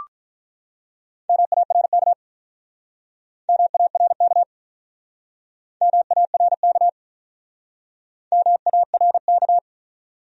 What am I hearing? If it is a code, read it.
Morse code, heard as a keyed tone, about 44 wpm, 700 Hz. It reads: MARK MARK MARK MARK
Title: Speed-Racing Top 500 Words 35wpm